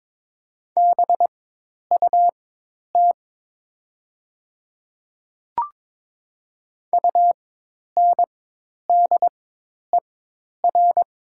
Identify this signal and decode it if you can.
Morse code, heard as a keyed tone, about 22 wpm, spaced as for 10 wpm, 700 Hz. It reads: BUT UNDER
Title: 4x Character Spacing - Top 500 Words - 22wpm